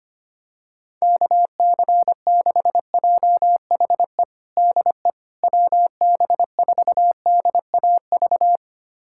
Morse code, heard as a keyed tone, about 25 wpm, 700 Hz. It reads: KC6JHE DE WB4DAV
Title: USA Callsign DE Callsign 25wpm